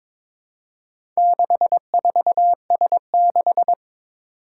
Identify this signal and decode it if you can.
Morse code, heard as a keyed tone, about 22 wpm, 700 Hz. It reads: 64S6
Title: Four Challenging Letter-Number Combinations 22wpm